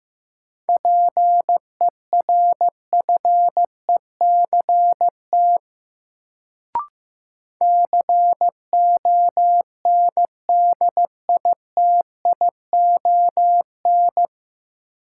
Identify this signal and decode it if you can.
Morse code, heard as a keyed tone, about 15 wpm, 700 Hz. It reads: PERFECT CONDITION
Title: Sentences from Top 1000 Words - New Word Drill - 15wpm